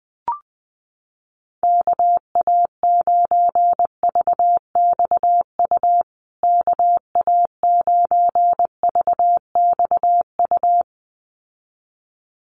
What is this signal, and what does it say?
Morse code, heard as a keyed tone, about 20 wpm, 700 Hz. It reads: KA9VXU KA9VXU
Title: USA Call Signs Repeated 2x 20wpm